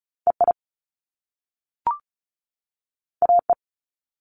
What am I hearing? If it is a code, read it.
Morse code, heard as a keyed tone, about 36 wpm, 700 Hz. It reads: EI AE
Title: Top Two Letter Combinations in Words 35wpm